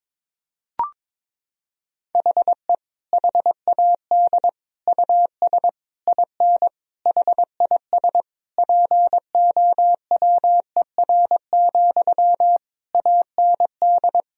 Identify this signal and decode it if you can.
Morse code, heard as a keyed tone, about 22 wpm, 700 Hz. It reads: HE HAD US IN HIS POWER, AND
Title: Sentences from Top 1000 Words - Review Part 2 - 22wpm